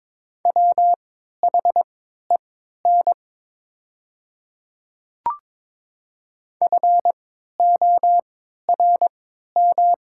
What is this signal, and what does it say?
Morse code, heard as a keyed tone, about 22 wpm, spaced as for 12 wpm, 700 Hz. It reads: WHEN FORM
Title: 3x Character Spacing - Top 300 Words - 22wpm